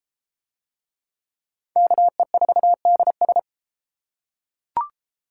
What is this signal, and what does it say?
Morse code, heard as a keyed tone, about 31 wpm, 700 Hz. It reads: KE4DS